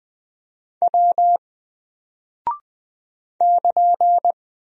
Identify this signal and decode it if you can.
Morse code, heard as a keyed tone, about 20 wpm, 700 Hz. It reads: W <KN>
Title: Single Character (Letter:Number:Punctuation:Prosign) 20wpm